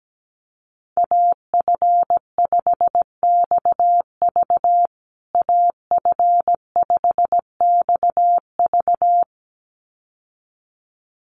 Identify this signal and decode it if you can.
Morse code, heard as a keyed tone, about 17 wpm, 700 Hz. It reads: AF5XV AF5XV
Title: USA Call Signs Repeated 2x 17wpm